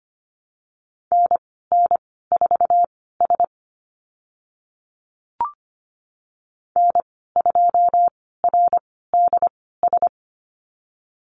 Morse code, heard as a keyed tone, about 25 wpm, spaced as for 16 wpm, 700 Hz. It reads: NN4S N2RDS
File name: Lesson 15g - Number 5 - Call Signs 25wpm